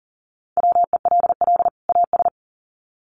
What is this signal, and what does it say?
Morse code, heard as a keyed tone, about 40 wpm, 700 Hz. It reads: WELL AS
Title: Sets of 2 Words 40wpm